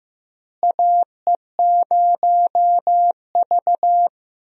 Morse code, heard as a keyed tone, about 15 wpm, 700 Hz. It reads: AE0V
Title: USA Call Signs - 4 Characters - Repeated 3x - 15wpm